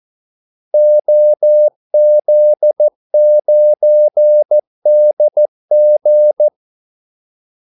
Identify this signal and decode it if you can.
Morse code, heard as a keyed tone, about 14 wpm, 600 Hz. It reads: OZ9DG
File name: Callsigns-14wpm